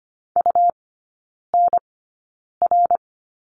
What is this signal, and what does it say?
Morse code, heard as a keyed tone, about 25 wpm, spaced as for 9 wpm, 700 Hz. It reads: UNR
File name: Lesson 17e - Letter u - Three Characters 25wpm